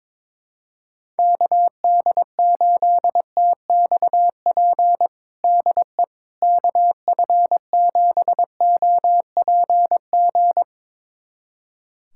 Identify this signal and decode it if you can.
Morse code, heard as a keyed tone, about 22 wpm, 700 Hz. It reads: KD8TXP DE KF7OPG
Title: USA Callsign DE Callsign - 6 Characters - 22wpm